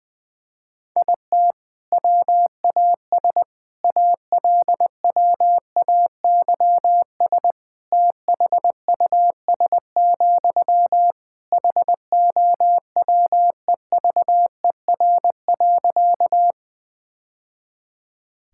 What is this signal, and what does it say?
Morse code, heard as a keyed tone, about 20 wpm, 700 Hz. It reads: IT WAS ALWAYS THUS, HOWEVER.
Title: Sentences from Top 1000 Words - Review Part 5 - 20wpm